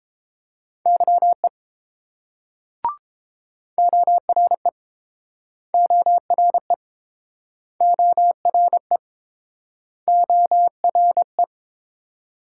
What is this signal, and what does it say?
Morse code, heard as a keyed tone, about 29 wpm, 700 Hz. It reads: YE ORE ORE ORE ORE